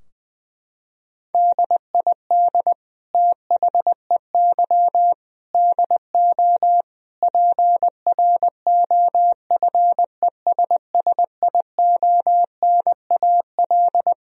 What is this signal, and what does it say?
Morse code, heard as a keyed tone, about 20 wpm, 700 Hz. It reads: DID THEY DO PROFESSIONAL
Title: Sentences from Top 1000 Words - Review Part 4 - 20wpm